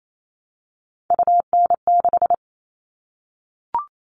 Morse code, heard as a keyed tone, about 28 wpm, 700 Hz. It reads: UN6
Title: All Cut Numbers - 3-Digit 28wpm